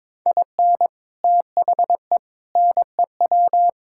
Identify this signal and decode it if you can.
Morse code, heard as a keyed tone, about 23 wpm, 700 Hz. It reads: IN THE NEW
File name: practice / 22wpm / Build A Phrase - 5-Word Phrases - 22wpm